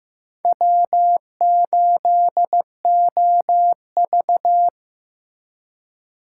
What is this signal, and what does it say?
Morse code, heard as a keyed tone, about 18 wpm, 700 Hz. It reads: W8OV
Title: Speed Racing - USA 1x2 Call Signs 15wpm